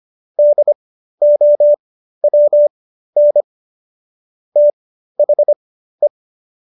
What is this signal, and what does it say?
Morse code, heard as a keyed tone, about 25 wpm, spaced as for 13 wpm, 590 Hz. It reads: DOWN THE